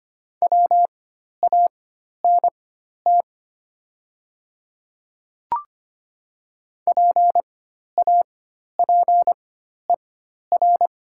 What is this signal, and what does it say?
Morse code, heard as a keyed tone, about 25 wpm, spaced as for 12 wpm, 700 Hz. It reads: WANT PAPER
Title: 4x Character Spacing - Top 300 Words - 25wpm